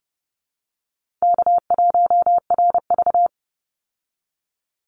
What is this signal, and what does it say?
Morse code, heard as a keyed tone, about 30 wpm, 700 Hz. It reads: K1RV
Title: Call Signs Repeated 3x - 30wpm